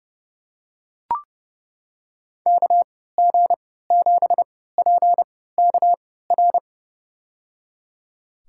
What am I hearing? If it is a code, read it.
Morse code, heard as a keyed tone, about 30 wpm, spaced as for 17 wpm, 700 Hz. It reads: KG7PKR
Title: USA Call Signs - 3x Character Spacing - 30wpm